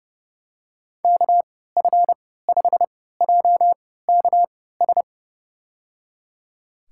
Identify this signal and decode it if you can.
Morse code, heard as a keyed tone, about 32 wpm, spaced as for 17 wpm, 700 Hz. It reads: KF5JKS
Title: USA Call Signs - 6 Characters - 3x Character Spacing - 30wpm